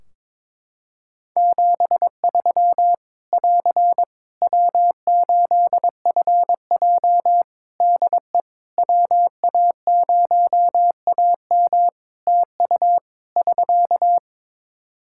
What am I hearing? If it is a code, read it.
Morse code, heard as a keyed tone, about 22 wpm, 700 Hz. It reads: 73 <AR> W8FJ DE WA0AM TU <SK>
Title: Practice QSO - Exchange 4 - Ending - 22wpm